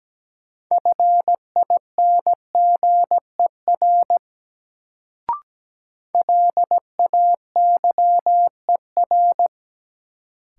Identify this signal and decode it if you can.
Morse code, heard as a keyed tone, about 17 wpm, 700 Hz. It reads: FINGER LAYER
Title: Sentences from Top 1600 Words - New Word Drill - 17wpm